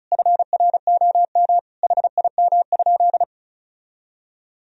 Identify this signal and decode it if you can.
Morse code, heard as a keyed tone, about 36 wpm, 700 Hz. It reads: FROM HIM?
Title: Sentences from Top 1000 Words - Review Part 2 - 35wpm